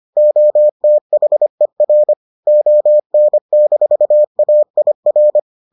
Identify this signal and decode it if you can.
Morse code, heard as a keyed tone, about 25 wpm, 600 Hz. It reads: OTHER ON-AIR